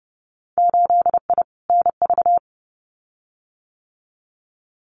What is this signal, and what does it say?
Morse code, heard as a keyed tone, about 30 wpm, 700 Hz. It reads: 8I NV
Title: Winter Field Day Exchanges Encore 30wpm